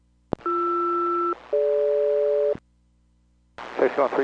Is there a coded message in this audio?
Yes: aircraft SELCAL from 0.5 to 2.5 s, heard as two pairs of tones, with speech. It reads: BR-DG